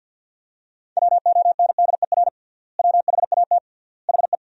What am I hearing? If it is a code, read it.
Morse code, heard as a keyed tone, about 46 wpm, 700 Hz. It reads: WONDER WHAT HE
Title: 3x Word-Spacing - Sentences from Top 500 Words - 50wpm